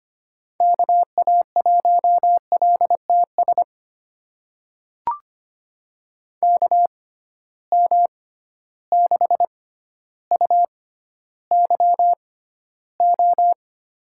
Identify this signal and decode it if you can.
Morse code, heard as a keyed tone, about 25 wpm, spaced as for 13 wpm, 700 Hz. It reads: KA1LTS KM6UYO
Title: USA Call Signs - 6 Characters - 6x Character Spacing - 25wpm